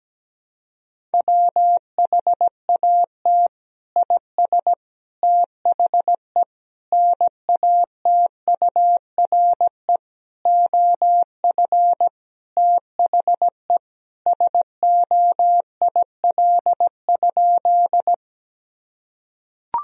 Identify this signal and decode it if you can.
Morse code, heard as a keyed tone, about 17 wpm, 700 Hz. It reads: WHAT IS THE NATURE OF THE SOIL?
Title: Sentences from Top 1000 Words - Review Part 4 - 17wpm